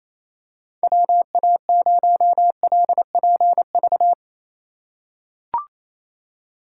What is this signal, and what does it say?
Morse code, heard as a keyed tone, about 27 wpm, 700 Hz. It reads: WA0LPV
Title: K1USN SST QSO Practice - 28wpm